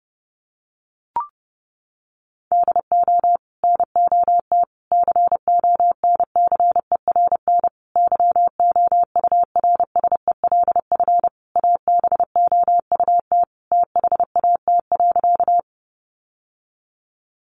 Morse code, heard as a keyed tone, about 30 wpm, 700 Hz. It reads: DO NOT CONCERN YOURSELF ABOUT THAT.
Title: Sentences from Top 1000 Words 30wpm